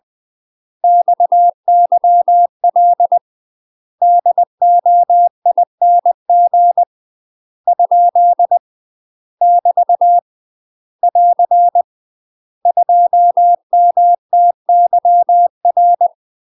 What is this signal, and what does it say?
Morse code, heard as a keyed tone, about 20 wpm, 700 Hz. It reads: XYL DOING ? <BT> <AR> 2MTYR
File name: UK MASTER QSOs TRAINING / 20WPM / UK MASTER QSOs 0069